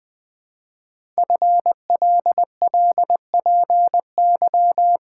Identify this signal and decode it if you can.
Morse code, heard as a keyed tone, about 20 wpm, 700 Hz. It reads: FLLPY
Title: Code Groups - 5 Letters - 20wpm